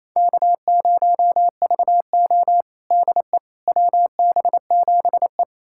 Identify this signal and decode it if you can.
Morse code, heard as a keyed tone, about 28 wpm, 700 Hz. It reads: K0VO DE WB7E
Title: USA Callsign DE Callsign - 4 Characters - 28wpm